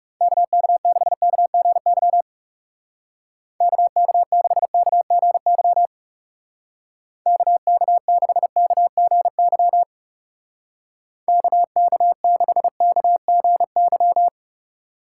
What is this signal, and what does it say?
Morse code, heard as a keyed tone, about 36 wpm, 700 Hz. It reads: KK6KGY KK6KGY KK6KGY KK6KGY
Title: Speed Racing - USA 2x3 Call Signs 30wpm